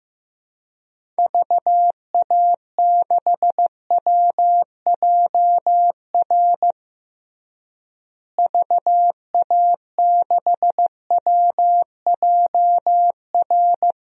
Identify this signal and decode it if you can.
Morse code, heard as a keyed tone, about 15 wpm, 700 Hz. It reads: VA6WJR VA6WJR
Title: DXCC-100 Call Signs - 6 Characters - Repeated 2x - 15wpm